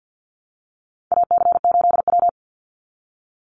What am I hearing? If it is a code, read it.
Morse code, heard as a keyed tone, about 48 wpm, 700 Hz. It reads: AC8W
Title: Call Signs Repeated 3x - 50wpm